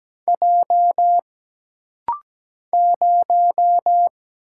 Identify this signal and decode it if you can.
Morse code, heard as a keyed tone, about 17 wpm, 700 Hz. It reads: J 0
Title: Single Letter-Number - Rapid-Fire 17wpm